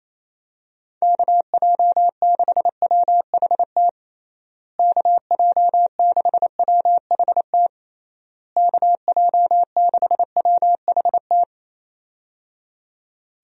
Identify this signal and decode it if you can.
Morse code, heard as a keyed tone, about 27 wpm, 700 Hz. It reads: KJ6WHT KJ6WHT KJ6WHT
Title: USA Call Signs - 6 Characters - Repeated 3x - 28wpm